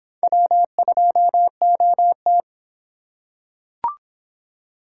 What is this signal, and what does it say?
Morse code, heard as a keyed tone, about 26 wpm, 700 Hz. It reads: W2OT